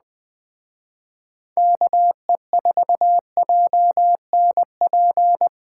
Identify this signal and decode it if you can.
Morse code, heard as a keyed tone, about 20 wpm, 700 Hz. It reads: KE4JNP
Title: Speed Racing - USA 2x3 Call Signs 17wpm